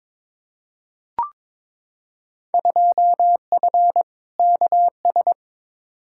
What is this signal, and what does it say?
Morse code, heard as a keyed tone, about 22 wpm, 700 Hz. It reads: 2F KS
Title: Field Day Exchanges - 22wpm